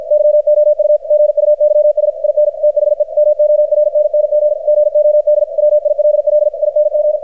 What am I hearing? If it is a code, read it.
Morse code, heard as a keyed tone, about 53 wpm, 600 Hz. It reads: YOU GUYS IN THE MORNNG GOD WILLING
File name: archived / CRSnet